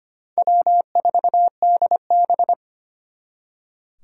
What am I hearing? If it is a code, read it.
Morse code, heard as a keyed tone, about 31 wpm, 700 Hz. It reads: W4DB